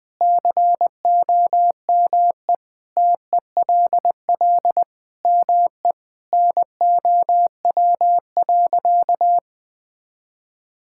Morse code, heard as a keyed tone, about 21 wpm, 700 Hz. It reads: COME TELL ME NOW.
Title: Sentences with 4-Word Phrases - 20wpm